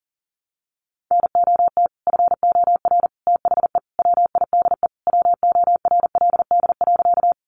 Morse code, heard as a keyed tone, about 40 wpm, 700 Hz. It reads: NOT FOR THE WIDE WORLD.
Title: Sentences from Top 900 Words 40wpm